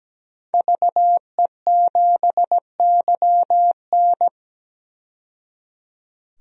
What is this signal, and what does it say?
Morse code, heard as a keyed tone, about 17 wpm, 700 Hz. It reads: VE7YN
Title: DXCC-100 Call Signs - 5 Characters - 17wpm